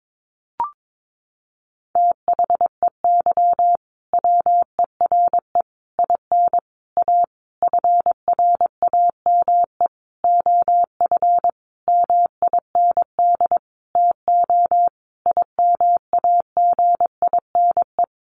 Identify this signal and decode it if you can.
Morse code, heard as a keyed tone, about 22 wpm, 700 Hz. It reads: THEY WERE IN A FRAME OF MIND TO IMAGINE
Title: Sentences from Top 1400 Words 22wpm